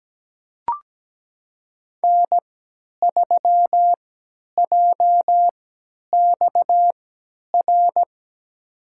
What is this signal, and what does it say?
Morse code, heard as a keyed tone, about 17 wpm, spaced as for 10 wpm, 700 Hz. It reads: N3JXR
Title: USA Call Signs - 3x Character Spacing - 17wpm